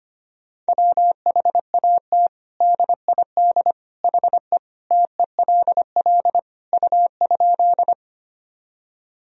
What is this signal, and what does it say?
Morse code, heard as a keyed tone, about 25 wpm, 700 Hz. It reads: WHAT DID HE TELL U?
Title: Sentences from Top 200 Words and Abbreviations 25wpm